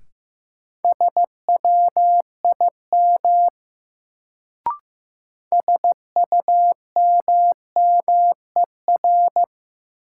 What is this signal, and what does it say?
Morse code, heard as a keyed tone, about 15 wpm, 700 Hz. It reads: SWIM SUMMER
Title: Sentences from Top 1000 Words - Review Part 6 - New Word Drill - 15wpm